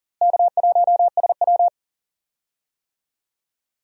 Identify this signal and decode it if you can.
Morse code, heard as a keyed tone, about 40 wpm, 700 Hz. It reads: K1SW
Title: USA Call Signs - 4 Characters - Repeated 3x - 40wpm